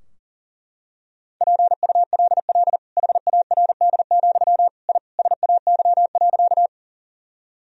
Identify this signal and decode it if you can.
Morse code, heard as a keyed tone, about 37 wpm, 700 Hz. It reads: PULL HARD, I SAY.
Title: Sentences from Top 1000 Words - Review Part 2 - 40wpm